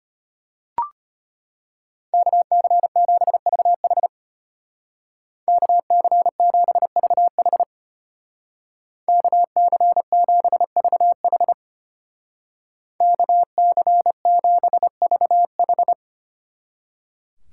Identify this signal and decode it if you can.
Morse code, heard as a keyed tone, about 32 wpm, 700 Hz. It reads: KC7VH KC7VH KC7VH KC7VH